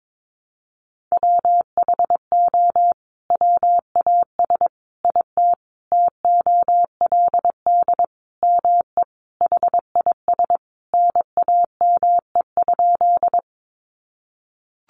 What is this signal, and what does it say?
Morse code, heard as a keyed tone, about 22 wpm, 700 Hz. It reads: WHO WAS IT TOLD ME HIS NAME?
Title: Sentences from Top 400 Words 22wpm